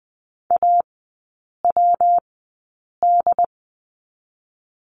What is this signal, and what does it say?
Morse code, heard as a keyed tone, about 20 wpm, 700 Hz. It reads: A W D
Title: 114 Lesson 19e - Letter w - Three Characters 20wpm